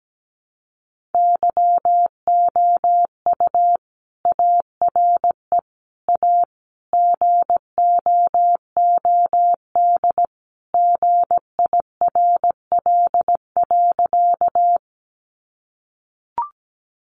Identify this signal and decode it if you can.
Morse code, heard as a keyed tone, about 17 wpm, 700 Hz. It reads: YOU ARE A GOOD GIRL.
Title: Sentences from Top 400 Words 17wpm